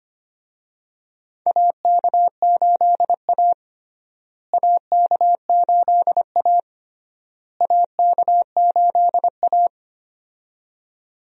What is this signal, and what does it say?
Morse code, heard as a keyed tone, about 25 wpm, 700 Hz. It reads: AK8A AK8A AK8A
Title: USA Call Signs - 4 Characters - Repeated 3x - 25wpm